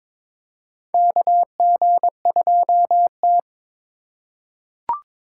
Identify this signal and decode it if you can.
Morse code, heard as a keyed tone, about 22 wpm, 700 Hz. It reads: KG2T
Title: USA Call Signs - 4 Characters - 2x Character Spacing - 22wpm